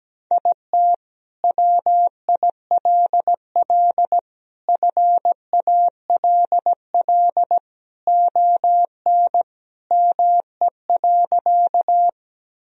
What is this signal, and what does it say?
Morse code, heard as a keyed tone, about 17 wpm, 700 Hz. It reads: IT WILL FALL ON ME.
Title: Sentences from Top 1000 Words - Review Part 2 - 17wpm